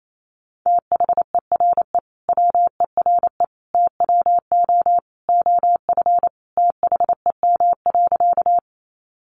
Morse code, heard as a keyed tone, about 28 wpm, 700 Hz. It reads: THERE WERE TWO OF THEM.
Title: Sentences from Top 100 Words 28wpm